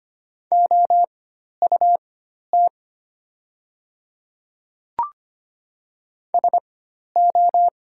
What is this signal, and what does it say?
Morse code, heard as a keyed tone, about 25 wpm, spaced as for 12 wpm, 700 Hz. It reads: OUT SO
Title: 4x Character Spacing - Top 100 Words - 25wpm